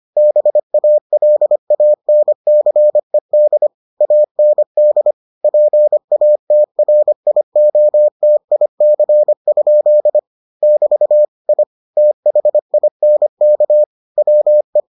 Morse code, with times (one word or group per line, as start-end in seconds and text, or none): 0.2-3.7 BALANCED
4.0-5.1 AND
5.4-10.2 PATRIOTIC?
10.6-11.2 <BT>
11.5-11.6 I
12.0-13.8 THINK
14.2-14.8 WE